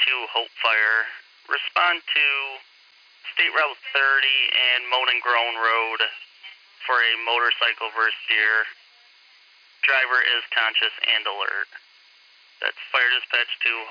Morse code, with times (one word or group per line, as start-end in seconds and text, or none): none